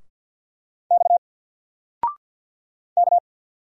Morse code, heard as a keyed tone, about 50 wpm, 700 Hz. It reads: X K
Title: Single Letter-Number Alt 50wpm